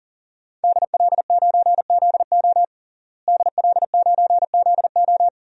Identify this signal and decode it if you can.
Morse code, heard as a keyed tone, about 37 wpm, 700 Hz. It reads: DL9ZO DL9ZO
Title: DXCC-100 Call Signs - 5 Characters - Repeated 2x - 40wpm